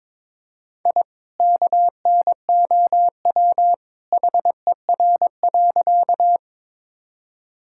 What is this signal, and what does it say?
Morse code, heard as a keyed tone, about 21 wpm, 700 Hz. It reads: I KNOW HER.
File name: practice / 22wpm / Sentences with 2-Word Phrases - 22wpm